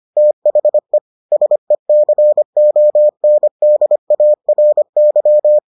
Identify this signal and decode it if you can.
Morse code, heard as a keyed tone, about 25 wpm, 600 Hz. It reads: THE SECONDARY